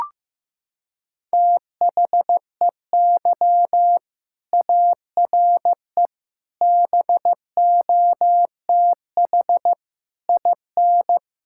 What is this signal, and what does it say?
Morse code, heard as a keyed tone, about 15 wpm, 700 Hz. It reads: THEY ARE BOTH IN
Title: Sentences from Top 1000 Words - Review Part 5 - 15wpm